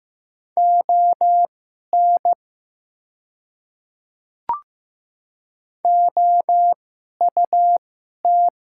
Morse code, heard as a keyed tone, about 15 wpm, spaced as for 11 wpm, 700 Hz. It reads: ON OUT
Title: 2x Character Spacing - Top 100 Words - 15wpm